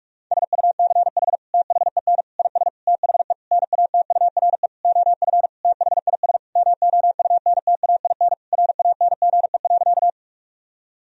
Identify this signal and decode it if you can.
Morse code, heard as a keyed tone, about 46 wpm, 700 Hz. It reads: SUCH THEN IS THE NATURE OF THIS MOUNTAIN RANGE.
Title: Sentences from Top 1000 Words - Review Part 4 - 45wpm